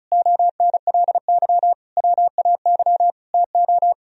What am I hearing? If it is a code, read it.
Morse code, heard as a keyed tone, about 42 wpm, 700 Hz. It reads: ONLY WAY TO
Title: Speed-Racing - Sets of 4 Words 35wpm